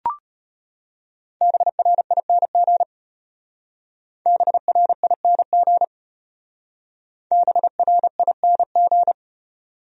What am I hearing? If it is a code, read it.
Morse code, heard as a keyed tone, about 33 wpm, 700 Hz. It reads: BRING BRING BRING